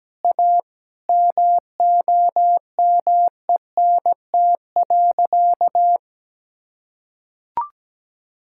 Morse code, heard as a keyed tone, about 17 wpm, 700 Hz. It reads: A MOMENT.
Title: Sentences from Top 1600 Words 17wpm